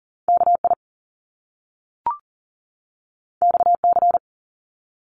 Morse code, heard as a keyed tone, about 39 wpm, 700 Hz. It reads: KI XC
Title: Top Two Letter Combinations in Words 40wpm